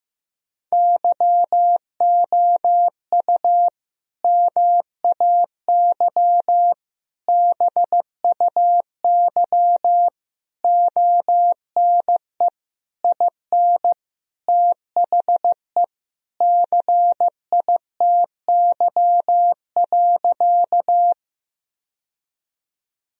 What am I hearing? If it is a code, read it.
Morse code, heard as a keyed tone, about 15 wpm, 700 Hz. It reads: YOU MAY BUY ONE IN THE CITY.
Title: Sentences from Top 1000 Words - Review Part 4 - 15wpm